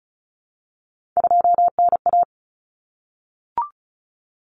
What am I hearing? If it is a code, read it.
Morse code, heard as a keyed tone, about 34 wpm, 700 Hz. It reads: 2NA
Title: Common Cut Numbers - 3-Digit 35wpm